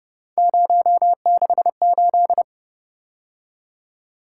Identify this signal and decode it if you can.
Morse code, heard as a keyed tone, about 30 wpm, 700 Hz. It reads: 068